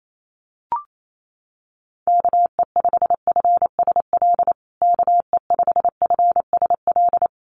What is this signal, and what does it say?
Morse code, heard as a keyed tone, about 28 wpm, 700 Hz. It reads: KE5FSL KE5FSL
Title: USA Call Signs Repeated 3x 28wpm